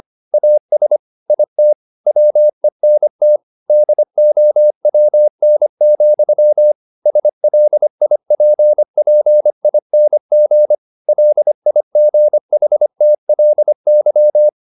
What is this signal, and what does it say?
Morse code, heard as a keyed tone, about 25 wpm, 600 Hz. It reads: AS IT WENT DOWN, SLIPPING LIGHTLY
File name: the old man and the sea part 1 25wpm